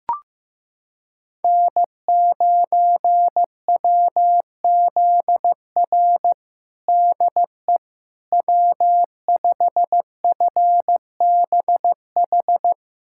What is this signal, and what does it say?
Morse code, heard as a keyed tone, about 15 wpm, 700 Hz. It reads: N9WZR DE W5FBH
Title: USA Callsign DE Callsign - 5 Characters - 15wpm